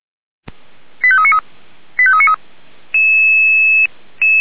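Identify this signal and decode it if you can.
Five-tone selective call (ZVEI), heard as quick runs of five tones, tones of about 80 ms. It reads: 85393 85393
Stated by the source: Five-tone ZVEI selective call received by SDR.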